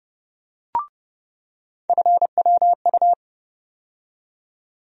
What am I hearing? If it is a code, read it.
Morse code, heard as a keyed tone, about 30 wpm, 700 Hz. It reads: FWU
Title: USA Call Sign Postfixes - 3 Letters 30wpm